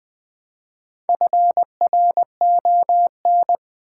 Morse code, heard as a keyed tone, about 20 wpm, 700 Hz. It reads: FRON
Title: Build A Word - 1 Letter At A Time - 5-Letter Words - 20wpm